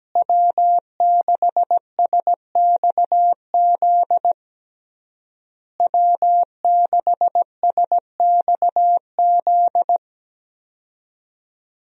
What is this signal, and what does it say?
Morse code, heard as a keyed tone, about 17 wpm, 700 Hz. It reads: W6SXZ W6SXZ
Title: USA Call Signs - 5 Characters - Repeated 3x - 17wpm